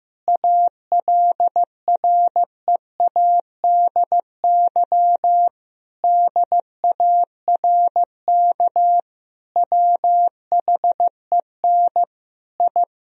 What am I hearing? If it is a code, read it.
Morse code, heard as a keyed tone, about 16 wpm, 700 Hz. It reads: ALREADY DARK WHEN I
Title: Sentences from Top 1000 Words - Review Part 2 - 15wpm